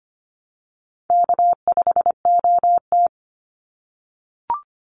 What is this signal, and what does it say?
Morse code, heard as a keyed tone, about 25 wpm, 700 Hz. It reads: K5OT
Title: NA SOTA Chaser Callsigns 25wpm